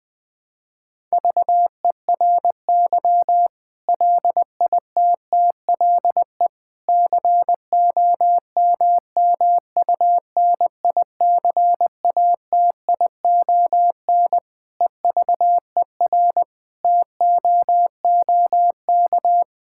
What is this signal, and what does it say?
Morse code, heard as a keyed tone, about 20 wpm, 700 Hz. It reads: VERY LITTLE COMMUNICATION EVER TOOK
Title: Sentences from Top 1000 Words - Review Part 6 - 20wpm